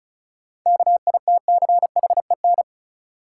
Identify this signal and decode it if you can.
Morse code, heard as a keyed tone, about 36 wpm, 700 Hz. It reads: KITCHEN
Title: Sentences from Top 1400 Words - New Word Drill Alt - 35wpm